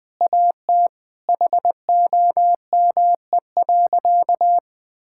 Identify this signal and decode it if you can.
Morse code, heard as a keyed tone, about 20 wpm, 700 Hz. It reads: AT HOME.
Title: Sentences from Top 1000 Words - Review Part 6 - 20wpm